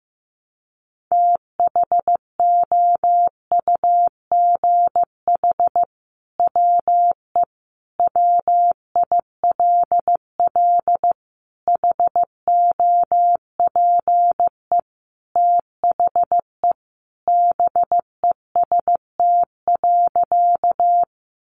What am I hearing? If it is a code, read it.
Morse code, heard as a keyed tone, about 15 wpm, 700 Hz. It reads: THOUGH WE WILL HOPE THE BEST.
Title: Sentences from Top 1000 Words 15wpm